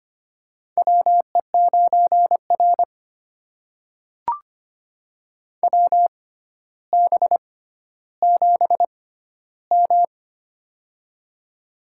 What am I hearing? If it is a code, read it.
Morse code, heard as a keyed tone, about 25 wpm, spaced as for 13 wpm, 700 Hz. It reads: WE9R WB7M
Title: USA Call Signs - 4 Characters - 6x Character Spacing - 25wpm